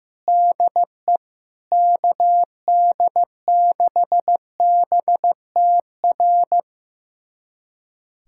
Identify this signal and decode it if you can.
Morse code, heard as a keyed tone, about 16 wpm, 700 Hz. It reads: DE KD6BTR
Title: USA Callsign DE Callsign 15wpm